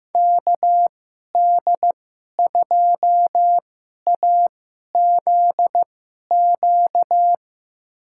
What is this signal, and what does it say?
Morse code, heard as a keyed tone, about 16 wpm, spaced as for 11 wpm, 700 Hz. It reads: KD2AZQ
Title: USA Call Signs - 6 Characters - 2x Character Spacing - 15wpm